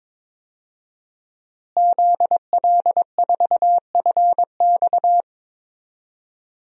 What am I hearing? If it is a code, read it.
Morse code, heard as a keyed tone, about 21 wpm, 700 Hz. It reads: ZL4FX
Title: DXCC-100 Call Signs - 5 Characters - Repeated 3x - 22wpm